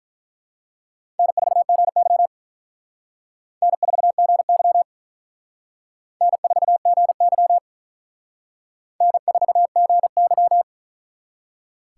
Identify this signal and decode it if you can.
Morse code, heard as a keyed tone, about 44 wpm, 700 Hz. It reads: N4GY N4GY N4GY N4GY